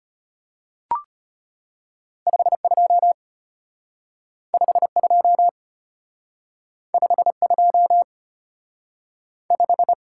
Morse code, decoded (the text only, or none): 52 52 52 5